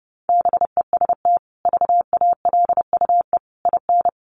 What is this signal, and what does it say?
Morse code, heard as a keyed tone, about 30 wpm, 700 Hz. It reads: BEST VALUE IN
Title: Sentences from Top 600 Words 30wpm